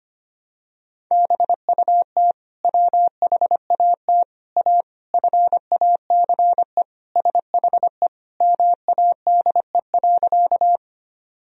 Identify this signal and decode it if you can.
Morse code, heard as a keyed tone, about 25 wpm, 700 Hz. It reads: BUT WHAT A FACE SHE MADE.
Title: Sentences from Top 1000 Words - Review Part 2 - 25wpm